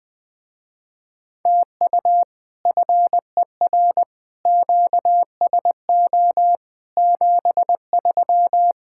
Morse code, heard as a keyed tone, about 20 wpm, 700 Hz. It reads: TU FER QSO 73
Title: Speed-Racing Common QSO Phrases Encore - 17wpm